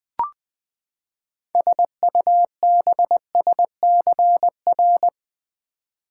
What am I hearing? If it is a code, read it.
Morse code, heard as a keyed tone, about 20 wpm, 700 Hz. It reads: SUBSCR
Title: Build A Word - 6 Letters At A Time - 12-Letter Words - 20wpm